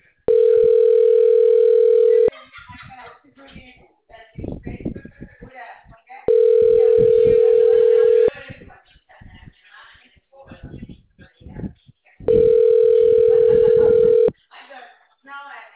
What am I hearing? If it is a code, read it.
Ringback tone (North America), 440 + 480 Hz, with speech.
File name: out- -2014- - - .22006